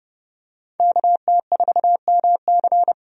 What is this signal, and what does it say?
Morse code, heard as a keyed tone, about 30 wpm, 700 Hz. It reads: KT4MC
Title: USA Call Signs - 5 Characters - 30wpm